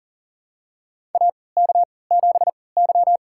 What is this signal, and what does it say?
Morse code, heard as a keyed tone, about 40 wpm, spaced as for 23 wpm, 700 Hz. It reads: AK7Y
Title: USA Call Signs - 3x Character Spacing - 40wpm